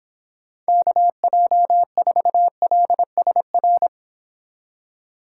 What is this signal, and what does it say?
Morse code, heard as a keyed tone, about 30 wpm, 700 Hz. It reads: KJ4LSR